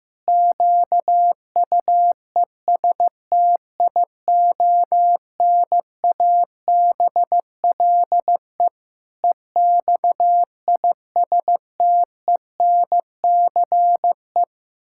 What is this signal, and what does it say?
Morse code, heard as a keyed tone, about 15 wpm, 700 Hz. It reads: QUESTIONABLE EXISTENCE
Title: Pangram Challenge v2 - 15wpm